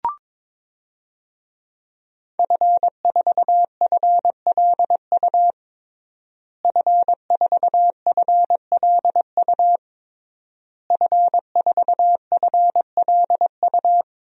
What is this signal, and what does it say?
Morse code, heard as a keyed tone, about 22 wpm, 700 Hz. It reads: F4FLU F4FLU F4FLU
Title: DXCC-100 Call Signs - 5 Characters - Repeated 3x - 22wpm